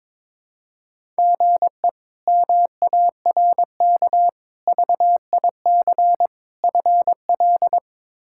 Morse code, heard as a keyed tone, about 22 wpm, 700 Hz. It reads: GE MARK VIC FL
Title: K1USN SST QSO Practice - 22wpm